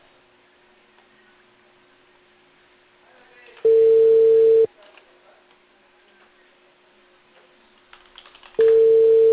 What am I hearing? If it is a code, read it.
Ringback tone (China, 450 Hz), with speech.